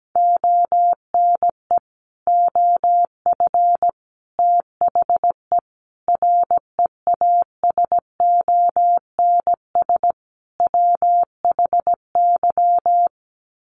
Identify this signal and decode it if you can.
Morse code, heard as a keyed tone, about 17 wpm, 700 Hz. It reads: ONE OF THE REASONS WHY
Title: Sets of 5 Words 17wpm